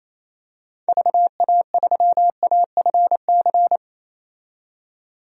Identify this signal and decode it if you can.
Morse code, heard as a keyed tone, about 27 wpm, 700 Hz. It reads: VA3AFC
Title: DXCC-100 Call Signs - 6 Characters - 28wpm